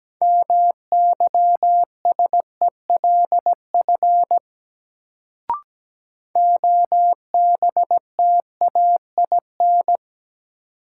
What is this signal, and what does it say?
Morse code, heard as a keyed tone, about 17 wpm, 700 Hz. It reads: MYSELF OBTAIN
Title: Sentences from Top 1300 Words - New Word Drill - 17wpm